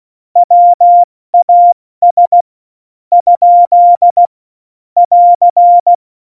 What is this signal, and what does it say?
Morse code, heard as a keyed tone, about 16 wpm, 700 Hz. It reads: WAS ? <AR>
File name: Morsekurs / Witze / Witz 19. Juni 2018, 70 F80